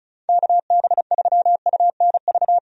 Morse code, heard as a keyed tone, about 34 wpm, 700 Hz. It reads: KB3UNV
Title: USA Call Signs - 6 Characters - 35wpm